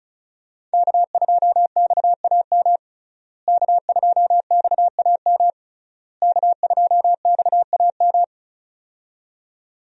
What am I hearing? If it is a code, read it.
Morse code, heard as a keyed tone, about 34 wpm, 700 Hz. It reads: K2XAM K2XAM K2XAM
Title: USA Call Signs - 5 Characters - Repeated 3x - 35wpm